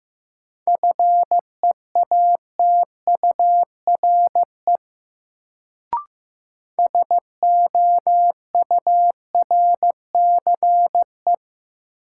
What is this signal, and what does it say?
Morse code, heard as a keyed tone, about 15 wpm, 700 Hz. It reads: FEATURE SOURCE
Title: Sentences from Top 800 Words - New Word Drill - 15wpm